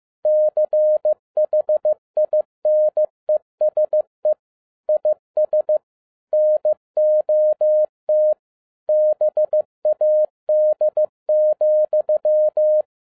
Morse code, heard as a keyed tone, about 15 wpm, 610 Hz. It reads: CHINESE IS NOT BAD,
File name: 2011-07-mp3 / n07 A LOT HAS CHANGED 15wpm